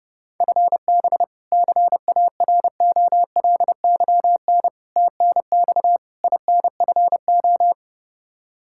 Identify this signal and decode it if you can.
Morse code, heard as a keyed tone, about 30 wpm, 700 Hz. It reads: FB CAROLYN TNX INFO
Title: Practice QSO - Exchange 3 - More Info - 30wpm